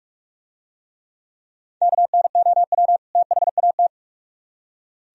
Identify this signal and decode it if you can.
Morse code, heard as a keyed tone, about 44 wpm, 700 Hz. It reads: KNOW THAT